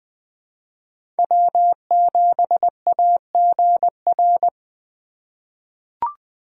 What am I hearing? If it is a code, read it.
Morse code, heard as a keyed tone, about 21 wpm, 700 Hz. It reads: W7AGR
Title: USA Call Signs - 5 Characters - 4x Character Spacing - 20wpm